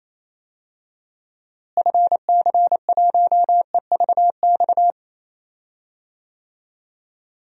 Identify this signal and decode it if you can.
Morse code, heard as a keyed tone, about 28 wpm, 700 Hz. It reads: FC1EVX
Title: DXCC-100 Call Signs - 6 Characters - Repeated 3x - 28wpm